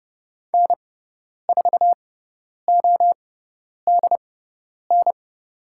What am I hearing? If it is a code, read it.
Morse code, heard as a keyed tone, about 30 wpm, spaced as for 10 wpm, 700 Hz. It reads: N4ODN
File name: Lesson 18g - Letter m - Call Signs 30wpm